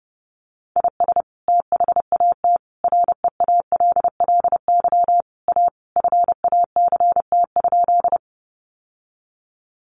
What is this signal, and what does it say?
Morse code, heard as a keyed tone, about 30 wpm, 700 Hz. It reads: IS THAT REALLY A FACT?
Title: Sentences from Top 600 Words 30wpm